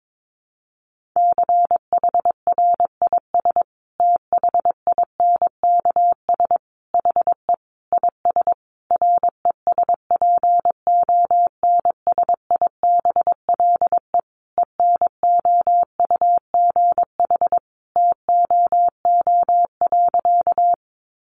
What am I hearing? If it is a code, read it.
Morse code, heard as a keyed tone, about 22 wpm, 700 Hz. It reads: CHRIS THINKS HE IS RESPONSIBLE ENOUGH TOO.
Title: Additional Easy Phrases 22wpm